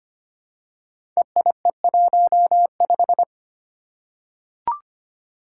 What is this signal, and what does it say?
Morse code, heard as a keyed tone, about 25 wpm, 700 Hz. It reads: EIE15
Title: Five Challenging Letter-Number Combinations 25wpm